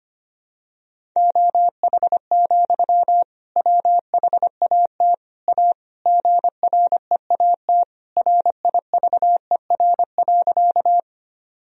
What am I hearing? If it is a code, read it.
Morse code, heard as a keyed tone, about 25 wpm, 700 Hz. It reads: OH, WHAT A GREAT RIVER.
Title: Sentences from Top 1000 Words - Review Part 1 - 25wpm